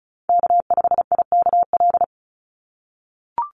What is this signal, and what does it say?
Morse code, heard as a keyed tone, about 34 wpm, 700 Hz. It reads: K5IKL
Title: USA Call Signs 35wpm